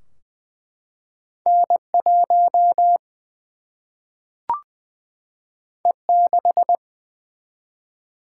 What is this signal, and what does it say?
Morse code, heard as a keyed tone, about 20 wpm, 700 Hz. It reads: N 1 E 6
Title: Single-Letter Single-Number 20wpm